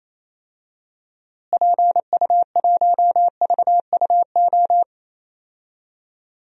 Morse code, heard as a keyed tone, about 28 wpm, 700 Hz. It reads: PU1VUO
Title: DXCC-100 Call Signs - 6 Characters - Repeated 3x - 28wpm